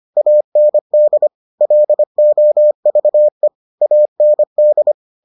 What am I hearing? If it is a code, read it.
Morse code, heard as a keyed tone, about 25 wpm, 600 Hz. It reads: AND LOVE AND